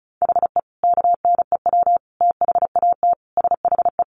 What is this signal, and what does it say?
Morse code, heard as a keyed tone, about 34 wpm, 700 Hz. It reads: HE KNEW THAT SHE
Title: Sentences from Top 1100 Words 35wpm